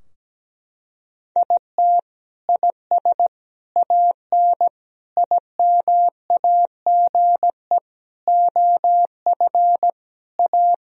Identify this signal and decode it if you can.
Morse code, heard as a keyed tone, about 17 wpm, 700 Hz. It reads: IT IS AN IMAGE OF A
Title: Sentences from Top 1000 Words - Review Part 4 - 17wpm